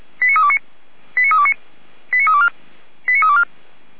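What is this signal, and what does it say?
Five-tone selective call (ZVEI), heard as quick runs of five tones, tones of about 70 ms. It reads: 89329 89329 89324 89324
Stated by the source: ZVEI five-tone selective call received on an SDR, from the Feuerwehr Orlamünde audio archive.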